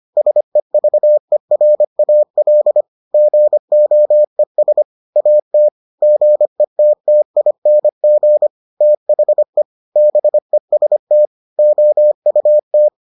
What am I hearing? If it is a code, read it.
Morse code, heard as a keyed tone, about 25 wpm, 600 Hz. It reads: SEVERAL GOES AT GETTING THE BEST OUT